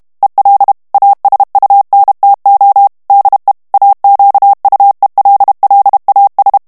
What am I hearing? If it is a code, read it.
Morse code, heard as a keyed tone, about 32 wpm, 800 Hz. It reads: EL ASUNTO DE AQUELLAS
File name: morse mp3 / ESPAGNOL TEXTE EN CLAIR 1 VITESSE 30 MOTS MINUTES